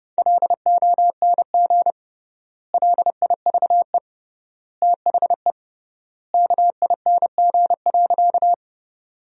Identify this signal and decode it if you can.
Morse code, heard as a keyed tone, about 30 wpm, 700 Hz. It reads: LONG LIVE THE KING.
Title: 3x Word-Spacing - Sentences from Top 400 Words - 30wpm